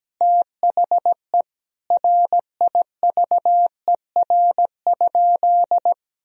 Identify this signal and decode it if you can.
Morse code, heard as a keyed tone, about 17 wpm, 700 Hz. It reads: THE RIVER?
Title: Sentences from Top 1000 Words - Review Part 1 - 17wpm